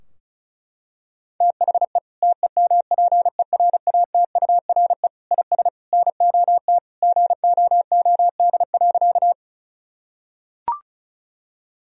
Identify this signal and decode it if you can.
Morse code, heard as a keyed tone, about 36 wpm, 700 Hz. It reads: THE TEMPERATURE IS NOT GOOD.
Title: Sentences from Top 1000 Words - Review Part 4 - 35wpm